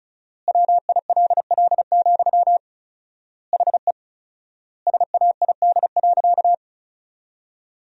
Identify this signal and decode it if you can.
Morse code, heard as a keyed tone, about 36 wpm, 700 Hz. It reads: WILL, HE SAID.
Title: 4x Word-Spacing - Sentences from Top 200 Words - 35wpm